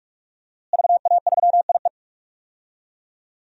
Morse code, heard as a keyed tone, about 46 wpm, 700 Hz. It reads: VA3IE
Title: DXCC-100 Call Signs - 5 Characters - Repeated 2x - 45wpm